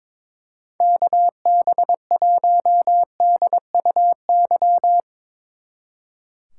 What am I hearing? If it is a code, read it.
Morse code, heard as a keyed tone, about 24 wpm, 700 Hz. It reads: KB1DUY
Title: Speed Racing - USA 2x3 Call Signs 22wpm